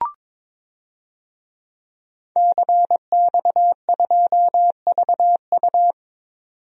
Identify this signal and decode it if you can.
Morse code, heard as a keyed tone, about 22 wpm, 700 Hz. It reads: CX2VU
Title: DXCC-100 Call Signs - 5 Characters - 22wpm